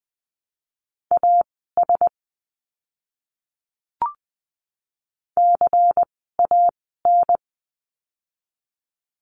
Morse code, heard as a keyed tone, about 20 wpm, spaced as for 15 wpm, 700 Hz. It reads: AS CAN
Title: 160 Lesson 26f - Letter g - Words 20wpm